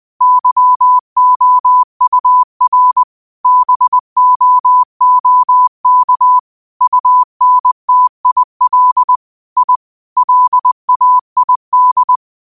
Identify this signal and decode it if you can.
Morse code, heard as a keyed tone, about 20 wpm, 1000 Hz. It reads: YOUR BOOK UNTIL I LAID
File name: qotdfiles / 20wpm